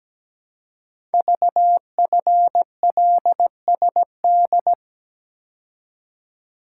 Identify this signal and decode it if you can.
Morse code, heard as a keyed tone, about 17 wpm, 700 Hz. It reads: VFLSD
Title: Code Groups - 5 Letters - 17wpm